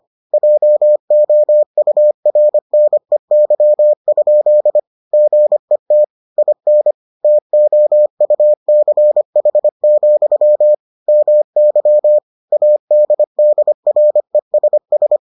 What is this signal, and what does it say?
Morse code, heard as a keyed tone, about 25 wpm, 600 Hz. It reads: JOURNEY? GET IN TOUCH, MY ADDRESS